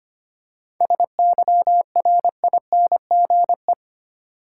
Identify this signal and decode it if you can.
Morse code, heard as a keyed tone, about 25 wpm, 690 Hz. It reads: SYRINGE
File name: practice / 25wpm / Difficult Words with Hints - 25wpm